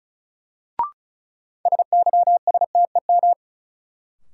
Morse code, heard as a keyed tone, about 36 wpm, 700 Hz. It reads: SYSTEM
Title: Sentences from Top 1000 Words - Review Part 3 - New Word Drill - 35wpm